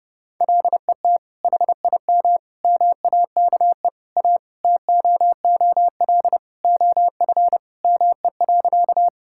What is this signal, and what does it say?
Morse code, heard as a keyed tone, about 32 wpm, 700 Hz. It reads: LET HIM MAKE A TOOL OF ME.
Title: Sentences from Top 1000 Words - Review Part 6 - 30wpm